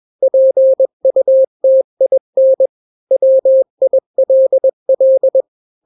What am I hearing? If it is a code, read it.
Morse code, heard as a keyed tone, about 21 wpm, 530 Hz. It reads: PUTIN WILL
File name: daily / 20wpm